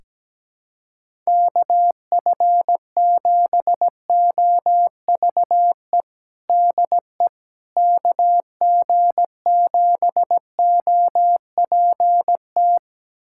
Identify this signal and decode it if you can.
Morse code, heard as a keyed tone, about 17 wpm, 700 Hz. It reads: KF7OVE DE KG7OPT
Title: USA Callsign DE Callsign - 6 Characters - 17wpm